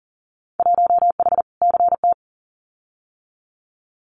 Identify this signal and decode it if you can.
Morse code, heard as a keyed tone, about 42 wpm, 700 Hz. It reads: 1H CT
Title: Winter Field Day Exchanges 40wpm